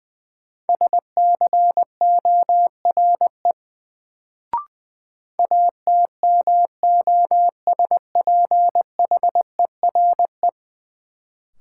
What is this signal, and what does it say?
Morse code, heard as a keyed tone, about 20 wpm, 700 Hz. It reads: SCORE ATMOSPHERE
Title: Sentences from Top 1600 Words - New Word Drill - 20wpm